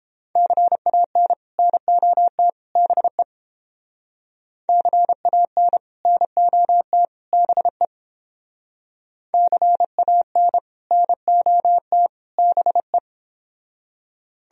Morse code, heard as a keyed tone, about 30 wpm, 700 Hz. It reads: CAN NOT BE CAN NOT BE CAN NOT BE